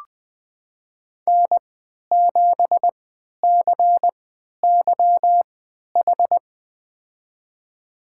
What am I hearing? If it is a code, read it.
Morse code, heard as a keyed tone, about 20 wpm, spaced as for 11 wpm, 700 Hz. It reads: N7CYH
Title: USA Call Signs - 5 Characters - 3x Character Spacing - 20wpm